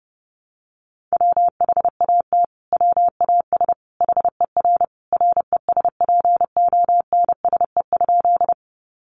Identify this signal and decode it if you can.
Morse code, heard as a keyed tone, about 30 wpm, 700 Hz. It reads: WHAT WAS HER RESPONSE?
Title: Sentences from Top 1400 Words 30wpm